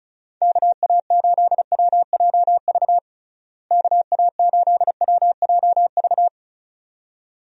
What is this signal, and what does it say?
Morse code, heard as a keyed tone, about 36 wpm, 700 Hz. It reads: KA8WJV KA8WJV
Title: USA Call Signs - 6 Characters - Repeated 2x - 35wpm